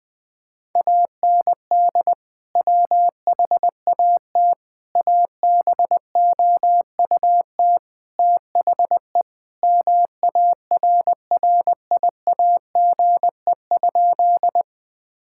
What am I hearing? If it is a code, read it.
Morse code, heard as a keyed tone, about 21 wpm, 700 Hz. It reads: AND WHAT ABOUT THE MARRIAGE?
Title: Sentences from Top 1000 Words - Review Part 5 - 20wpm